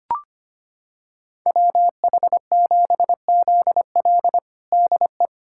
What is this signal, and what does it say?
Morse code, heard as a keyed tone, about 25 wpm, 700 Hz. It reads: WH7ZL DE
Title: USA Callsign DE Callsign - 5 Characters - 25wpm